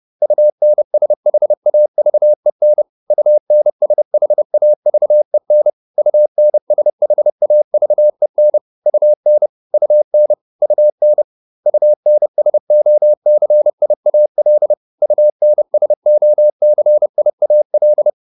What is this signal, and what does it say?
Morse code, heard as a keyed tone, about 30 wpm, 600 Hz. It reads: UNSHAVEN UNSHAVEN UNSHAVEN UN UN UN UNSOCIAL UNSOCIAL